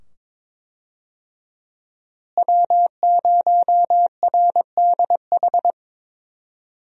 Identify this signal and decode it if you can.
Morse code, heard as a keyed tone, about 21 wpm, 700 Hz. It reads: W0RDH
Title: USA Call Signs - 5 Characters - Repeated 2x - 22wpm